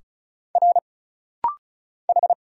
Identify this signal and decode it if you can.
Morse code, heard as a keyed tone, about 34 wpm, 700 Hz. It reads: R H
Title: Single Letters - Rapid-Fire 35wpm